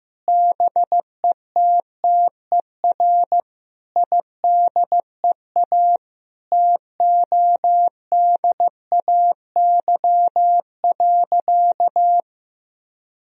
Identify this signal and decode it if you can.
Morse code, heard as a keyed tone, about 15 wpm, 700 Hz. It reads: BETTER IDEA TODAY.
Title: Sentences from Top 1000 Words - Review Part 4 - 15wpm